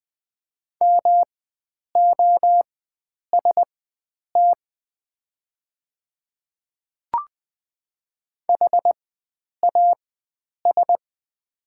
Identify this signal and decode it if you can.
Morse code, heard as a keyed tone, about 20 wpm, spaced as for 9 wpm, 700 Hz. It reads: MOST HAS
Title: 4x Character Spacing - Top 100 Words - 20wpm